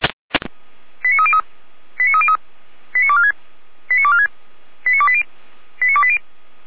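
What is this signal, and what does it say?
Five-tone selective call (ZVEI), heard as quick runs of five tones, tones of about 70 ms. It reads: 89303 89303 89246 89246 89280 89280
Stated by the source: ZVEI five-tone selective call received by SDR, logged on a Feuerwehr Orlamünde fire brigade server.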